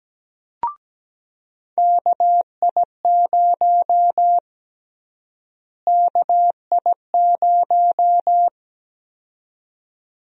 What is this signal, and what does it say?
Morse code, heard as a keyed tone, about 17 wpm, 700 Hz. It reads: KI0 KI0
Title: USA Call Sign Prefixes - Repeated 2x - 17wpm